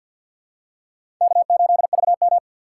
Morse code, heard as a keyed tone, about 52 wpm, 700 Hz. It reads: K8VM
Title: USA Call Signs - 4 Characters - 6x Character Spacing - 50wpm